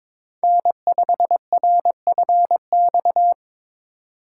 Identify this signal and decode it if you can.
Morse code, heard as a keyed tone, about 22 wpm, 700 Hz. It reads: N5RFX
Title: USA Call Signs - 5 Characters - 2x Character Spacing - 22wpm